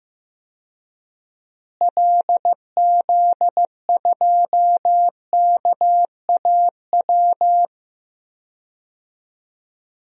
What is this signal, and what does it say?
Morse code, heard as a keyed tone, about 15 wpm, 700 Hz. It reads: LZ2KAW
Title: DXCC-100 Call Signs - 6 Characters - Repeated 2x - 15wpm